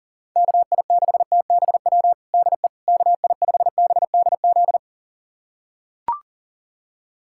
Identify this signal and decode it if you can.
Morse code, heard as a keyed tone, about 40 wpm, 700 Hz. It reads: KI6TBW DE KI5BDZ
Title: USA Callsign DE Callsign - 6 Characters - 40wpm